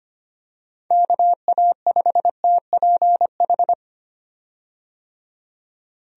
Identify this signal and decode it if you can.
Morse code, heard as a keyed tone, about 25 wpm, 700 Hz. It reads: KA5TPH
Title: USA Call Signs - 6 Characters - 25wpm